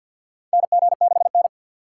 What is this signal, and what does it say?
Morse code, heard as a keyed tone, about 52 wpm, 700 Hz. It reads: NG6N
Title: USA Call Signs - 4 Characters - 6x Character Spacing - 50wpm